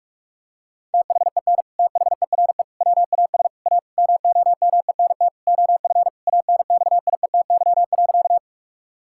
Morse code, heard as a keyed tone, about 41 wpm, 700 Hz. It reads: THEN THERE WAS A MOMENT OF ANXIETY.
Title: Sentences from Top 1600 Words 45wpm